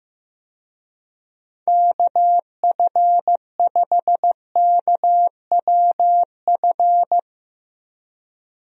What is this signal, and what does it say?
Morse code, heard as a keyed tone, about 15 wpm, 700 Hz. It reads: KF5KWF
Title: USA Call Signs - 6 Characters - Repeated 2x - 15wpm